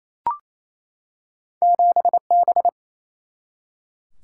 Morse code, heard as a keyed tone, about 28 wpm, 700 Hz. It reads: 7B
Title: Single-Letter Single-Number 28wpm